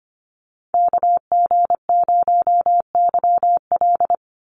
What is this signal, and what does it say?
Morse code, heard as a keyed tone, about 25 wpm, 700 Hz. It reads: KG0YL
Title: Call Signs - 25wpm